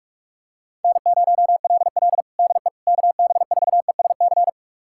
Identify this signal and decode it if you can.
Morse code, heard as a keyed tone, about 46 wpm, 700 Hz. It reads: N0LL DE KB4ESC
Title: USA Callsign DE Callsign 45wpm